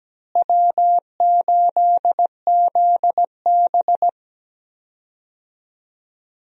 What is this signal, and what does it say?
Morse code, heard as a keyed tone, about 17 wpm, 700 Hz. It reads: W8ZB
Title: USA Call Signs - 4 Characters - Repeated 3x - 17wpm